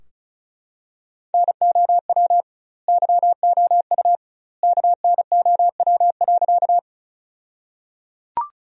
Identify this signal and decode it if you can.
Morse code, heard as a keyed tone, about 36 wpm, 700 Hz. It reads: NOW YOU KNOW.
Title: 2x Word-Spacing - Sentences from Top 100 Words - 35wpm